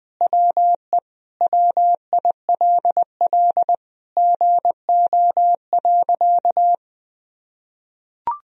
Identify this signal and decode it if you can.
Morse code, heard as a keyed tone, about 21 wpm, 700 Hz. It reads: WE WILL GO.
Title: Sentences from Top 1000 Words - Review Part 1 - 20wpm